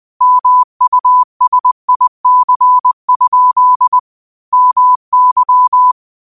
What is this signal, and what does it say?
Morse code, heard as a keyed tone, about 20 wpm, 1000 Hz. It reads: MUSIC? MY
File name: qotdfiles / 20wpm